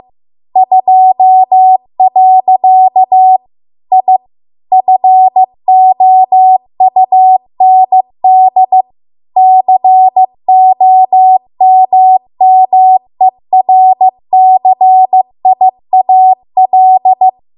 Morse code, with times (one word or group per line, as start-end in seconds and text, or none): 0.6-3.4 2.
3.9-4.2 I
4.7-8.8 FOUND
9.4-17.3 COMMERCIAL